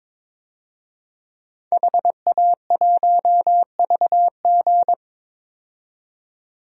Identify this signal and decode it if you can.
Morse code, heard as a keyed tone, about 22 wpm, 700 Hz. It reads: HA1VG
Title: DXCC-100 Call Signs - 5 Characters - 22wpm